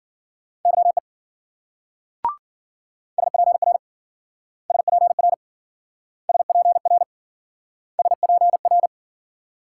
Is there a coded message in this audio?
Yes: Morse code from 0.7 to 8.9 s, heard as a keyed tone, about 54 wpm, 700 Hz. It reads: YE SPR SPR SPR SPR